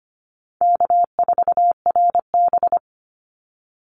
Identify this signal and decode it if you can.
Morse code, heard as a keyed tone, about 25 wpm, 700 Hz. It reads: K4RB
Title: Easy Call Signs Repeated 2x - 25wpm